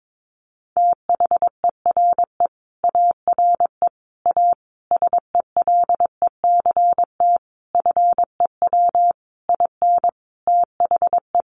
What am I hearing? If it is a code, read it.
Morse code, heard as a keyed tone, about 22 wpm, 700 Hz. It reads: THERE ARE A SELECT FEW IN THE
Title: Sentences from Top 1300 Words 22wpm